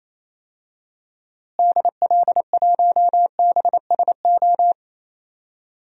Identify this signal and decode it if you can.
Morse code, heard as a keyed tone, about 28 wpm, 700 Hz. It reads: DL1BSO
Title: DXCC-100 Call Signs - 6 Characters - Repeated 2x - 28wpm